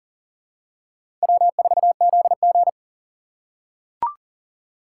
Morse code, heard as a keyed tone, about 39 wpm, 700 Hz. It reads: W4ZG
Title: USA Call Signs - 4 Characters - 3x Character Spacing - 40wpm